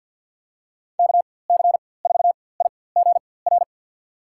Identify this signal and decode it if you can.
Morse code, heard as a keyed tone, about 49 wpm, spaced as for 23 wpm, 700 Hz. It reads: KC4IGR
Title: USA Call Signs - 6 Characters - 4x Character Spacing - 50wpm